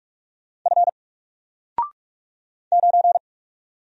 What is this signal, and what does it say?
Morse code, heard as a keyed tone, about 44 wpm, 700 Hz. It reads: F 9
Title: Single Character (Letter:Number:Punctuation:Prosign) 45wpm